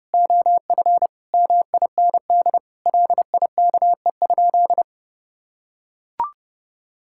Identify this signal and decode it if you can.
Morse code, heard as a keyed tone, about 32 wpm, 700 Hz. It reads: OF MIND LIKE?
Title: Sentences from Top 1000 Words - Review Part 6 - 30wpm